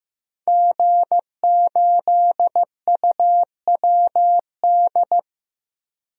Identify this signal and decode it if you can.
Morse code, heard as a keyed tone, about 15 wpm, 700 Hz. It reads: G8UWD
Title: DXCC-100 Call Signs - 5 Characters - Repeated 2x - 15wpm